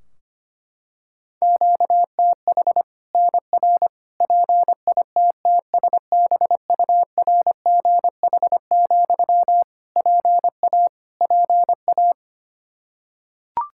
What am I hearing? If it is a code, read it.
Morse code, heard as a keyed tone, about 25 wpm, 700 Hz. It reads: QTH NR PITTSBURGH, PA PA
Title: Practice QSO - Exchange 1 - Beginning - 25wpm